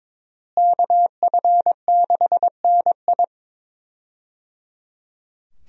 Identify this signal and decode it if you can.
Morse code, heard as a keyed tone, about 22 wpm, 700 Hz. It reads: KF6NI
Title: USA Call Signs - 5 Characters - Repeated 3x - 22wpm